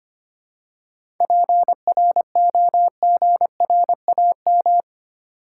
Morse code, heard as a keyed tone, about 25 wpm, 700 Hz. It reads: PROGRAM
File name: practice / 25wpm / Easy Words with Hints - 25wpm